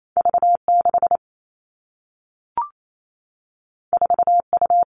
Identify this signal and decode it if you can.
Morse code, heard as a keyed tone, about 28 wpm, 700 Hz. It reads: V 6 4 U
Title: Two Challenging Letter-Number Combinations 28wpm